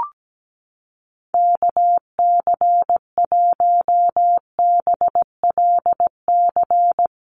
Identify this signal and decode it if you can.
Morse code, heard as a keyed tone, about 17 wpm, 700 Hz. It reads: KC1BLC
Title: USA Call Signs Repeated 3x 17wpm